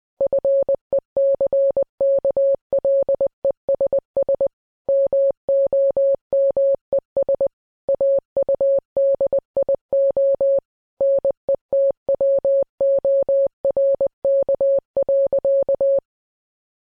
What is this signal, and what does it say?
Morse code, heard as a keyed tone, about 20 wpm, 550 Hz. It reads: FECKLESS MOMES AUDIO NETWORK.